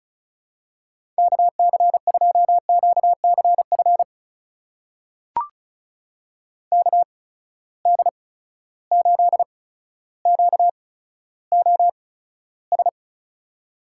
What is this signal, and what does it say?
Morse code, heard as a keyed tone, about 36 wpm, spaced as for 15 wpm, 700 Hz. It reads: KC2QCF KD8QOS
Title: USA Call Signs - 6 Characters - 8x Character Spacing - 35wpm